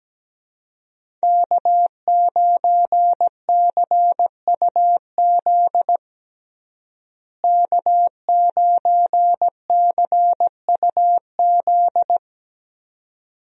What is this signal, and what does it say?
Morse code, heard as a keyed tone, about 17 wpm, 700 Hz. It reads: K9CUZ K9CUZ
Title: USA Call Signs - 5 Characters - Repeated 2x - 17wpm